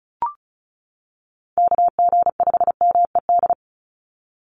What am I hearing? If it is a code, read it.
Morse code, heard as a keyed tone, about 34 wpm, 700 Hz. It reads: KG5MED
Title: USA Call Signs 35wpm